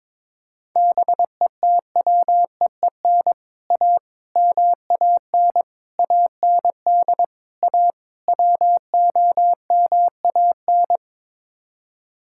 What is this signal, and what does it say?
Morse code, heard as a keyed tone, about 22 wpm, 700 Hz. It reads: BETWEEN A MAN AND A WOMAN
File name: practice / 22wpm / Sets of 6 Words - 22wpm